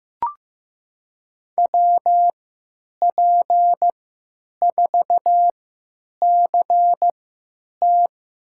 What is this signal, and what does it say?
Morse code, heard as a keyed tone, about 15 wpm, spaced as for 9 wpm, 700 Hz. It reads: WP4CT
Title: USA Call Signs - 3x Character Spacing - 15wpm